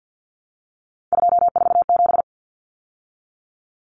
Morse code, heard as a keyed tone, about 50 wpm, 700 Hz. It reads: 247
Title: Numbers - 3-Digit 50wpm